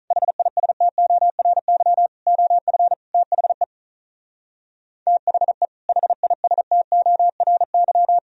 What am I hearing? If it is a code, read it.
Morse code, heard as a keyed tone, about 43 wpm, 700 Hz. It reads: HISTORY OF THE THE HISTORY
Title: Speed-Racing - Sets of 4 Words 35wpm